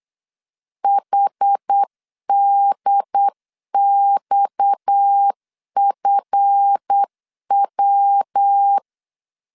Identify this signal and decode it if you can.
Morse code, heard as a keyed tone, about 8 wpm, 800 Hz. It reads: HDXFW
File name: LEZIONI / 7 GF9Z / 7 GF 8wpm 3108